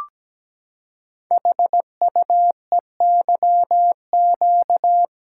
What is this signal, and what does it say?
Morse code, heard as a keyed tone, about 17 wpm, 700 Hz. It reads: HUEYQ
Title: Code Groups - 5 Letters - 17wpm